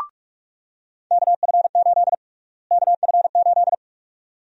Morse code, heard as a keyed tone, about 45 wpm, 700 Hz. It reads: KF8 KF8
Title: USA Call Sign Prefixes - Repeated 2x - 45wpm